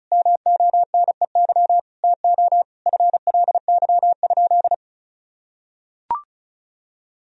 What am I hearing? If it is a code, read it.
Morse code, heard as a keyed tone, about 34 wpm, 700 Hz. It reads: MONEY TO FLY?
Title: Sentences from Top 1000 Words - Review Part 2 - 35wpm